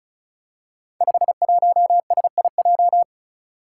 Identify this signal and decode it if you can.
Morse code, heard as a keyed tone, about 36 wpm, 700 Hz. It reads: 51SIJ
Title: Five Challenging Letter-Number Combinations 35wpm